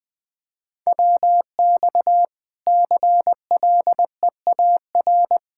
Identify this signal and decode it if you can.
Morse code, heard as a keyed tone, about 20 wpm, 700 Hz. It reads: WX CLEAR
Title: K1USN SST QSO Practice 20wpm